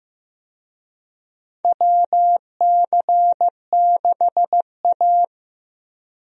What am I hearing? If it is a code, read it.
Morse code, heard as a keyed tone, about 15 wpm, 700 Hz. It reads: WC6A
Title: USA Call Signs - 4 Characters - Repeated 3x - 15wpm